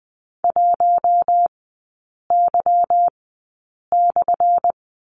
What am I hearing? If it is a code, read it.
Morse code, heard as a keyed tone, about 20 wpm, spaced as for 8 wpm, 700 Hz. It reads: 1Y/
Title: 178 Lesson 29e - Slash - Three Characters 20wpm